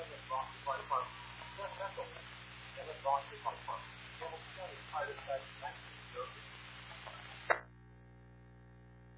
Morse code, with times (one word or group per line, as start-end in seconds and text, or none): none